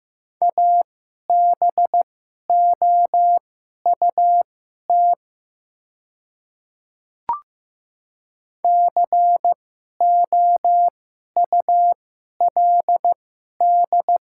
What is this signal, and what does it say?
Morse code, heard as a keyed tone, about 15 wpm, spaced as for 11 wpm, 700 Hz. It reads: ABOUT COULD
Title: 2x Character Spacing - Top 100 Words - 15wpm